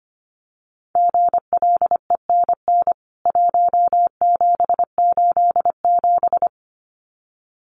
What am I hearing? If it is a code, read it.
Morse code, heard as a keyed tone, about 25 wpm, 700 Hz. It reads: GLENN 1787
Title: CWT Exchange 25wpm